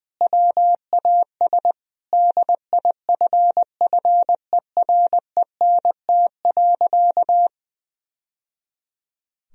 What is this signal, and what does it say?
Morse code, heard as a keyed tone, about 20 wpm, 700 Hz. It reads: WAS DIFFERENT.
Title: Sentences from Top 1000 Words - Review Part 4 - 20wpm